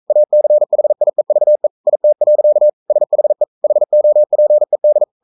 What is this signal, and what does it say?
Morse code, heard as a keyed tone, about 42 wpm, 600 Hz. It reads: ACHIEVE IT. SHE HOPED
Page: Romanen Never af Ken Follet på engelsk.